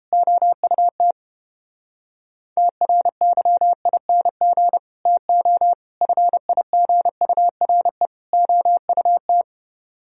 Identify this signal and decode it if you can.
Morse code, heard as a keyed tone, about 29 wpm, 700 Hz. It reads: OUT TRYING TO FIGURE OUT